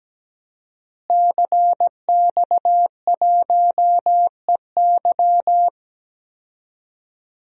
Morse code, heard as a keyed tone, about 17 wpm, 700 Hz. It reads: CX1EY
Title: DXCC-100 Call Signs - 5 Characters - Repeated 3x - 17wpm